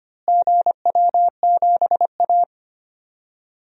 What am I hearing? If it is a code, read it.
Morse code, heard as a keyed tone, about 25 wpm, 700 Hz. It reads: GW7A
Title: DXCC-100 Call Signs - 4 Characters - Repeated 2x - 25wpm